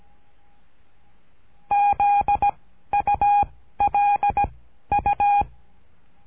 Morse code, heard as a keyed tone, about 17 wpm, 820 Hz. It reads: Z U L U
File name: cursos / CURSO DE TELEGRAFIA / Telegrafia - Trilha 13